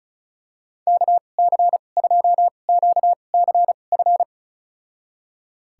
Morse code, heard as a keyed tone, about 36 wpm, spaced as for 25 wpm, 700 Hz. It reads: KC2QCF
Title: USA Call Signs - 6 Characters - 2x Character Spacing - 35wpm